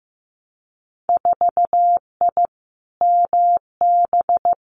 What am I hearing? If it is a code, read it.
Morse code, heard as a keyed tone, about 15 wpm, 700 Hz. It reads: 4I MB
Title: Winter Field Day Exchanges Encore 15wpm